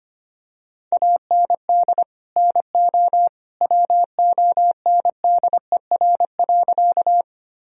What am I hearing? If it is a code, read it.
Morse code, heard as a keyed tone, about 25 wpm, 700 Hz. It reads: AND NO WONDER.
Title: Sentences from Top 1000 Words - Review Part 3 - 25wpm